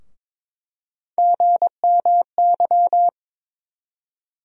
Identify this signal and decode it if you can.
Morse code, heard as a keyed tone, about 21 wpm, 700 Hz. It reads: GMY
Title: USA Call Sign Postfixes - 3 Letters 22wpm